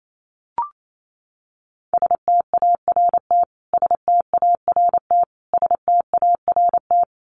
Morse code, heard as a keyed tone, about 28 wpm, 700 Hz. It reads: START START START
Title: Top 500 Words Repeated 28wpm